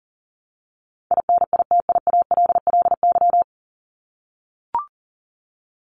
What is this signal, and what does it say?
Morse code, heard as a keyed tone, about 40 wpm, 700 Hz. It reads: INITIALLY
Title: Top 1251-1500 Words 40wpm